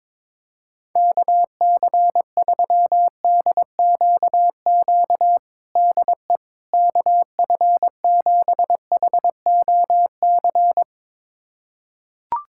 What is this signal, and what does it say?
Morse code, heard as a keyed tone, about 22 wpm, 700 Hz. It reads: KC3DQQ DE KF7HOC
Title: USA Callsign DE Callsign 22wpm